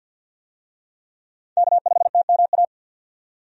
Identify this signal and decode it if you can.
Morse code, heard as a keyed tone, about 52 wpm, 700 Hz. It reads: K5TMA
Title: USA Call Signs - 5 Characters - 50wpm